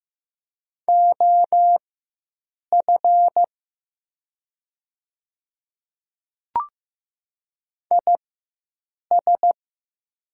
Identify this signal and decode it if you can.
Morse code, heard as a keyed tone, about 15 wpm, spaced as for 7 wpm, 700 Hz. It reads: OF IS
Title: 4x Character Spacing - Top 200 Words - 15wpm